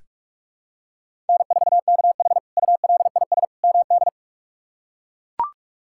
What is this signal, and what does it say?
Morse code, heard as a keyed tone, about 41 wpm, 700 Hz. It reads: N4KH ULIS MD
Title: K1USN SST QSO Practice - 45wpm